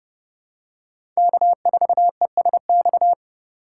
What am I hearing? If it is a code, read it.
Morse code, heard as a keyed tone, about 30 wpm, 700 Hz. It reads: K4ESX
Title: USA Call Signs - 5 Characters - 4x Character Spacing - 30wpm